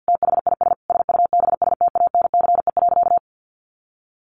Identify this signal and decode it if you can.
Morse code, heard as a keyed tone, about 50 wpm, 700 Hz. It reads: THIS SUBSTANCE.
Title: Sentences from Top 1500 Words 50wpm